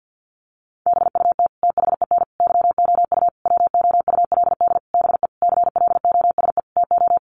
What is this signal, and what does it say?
Morse code, heard as a keyed tone, about 50 wpm, 700 Hz. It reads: BUT THEN YOU WOULD BE CLOSE TO
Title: Sentences from Top 300 Words 50wpm